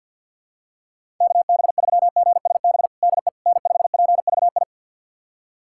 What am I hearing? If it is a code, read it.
Morse code, heard as a keyed tone, about 52 wpm, 700 Hz. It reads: KB3GSB DE N5PVI
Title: USA Callsign DE Callsign 50wpm